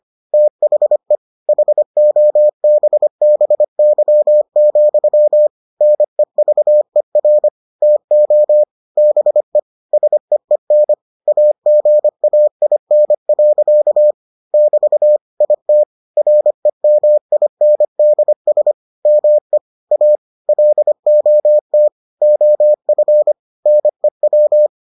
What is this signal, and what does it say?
Morse code, heard as a keyed tone, about 25 wpm, 600 Hz. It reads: THE HOBBY, NEVER TO BE SEEN AGAIN. <BT> IT REMINDS ME A LOT OF NEW